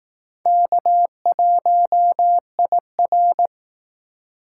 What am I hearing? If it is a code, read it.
Morse code, heard as a keyed tone, about 18 wpm, 700 Hz. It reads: K1IR
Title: Speed Racing - USA 1x2 Call Signs 15wpm